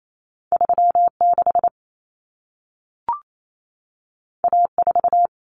Morse code, heard as a keyed tone, about 28 wpm, 700 Hz. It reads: 36 A4
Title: Common Cut Numbers - 2-Digit 28wpm